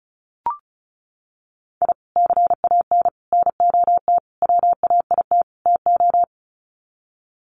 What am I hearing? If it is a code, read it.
Morse code, heard as a keyed tone, about 34 wpm, 700 Hz. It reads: I CAN NOT WAIT TO
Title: Sets of 5 Words Encore 35wpm